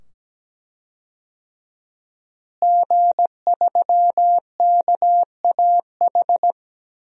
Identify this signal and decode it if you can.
Morse code, heard as a keyed tone, about 17 wpm, 700 Hz. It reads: G3KAH
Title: DXCC-100 Call Signs - 5 Characters - Repeated 2x - 17wpm